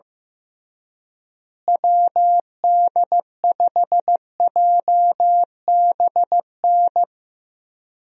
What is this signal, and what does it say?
Morse code, heard as a keyed tone, about 15 wpm, 700 Hz. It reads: WD5JBN
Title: USA Call Signs - 6 Characters - Repeated 2x - 15wpm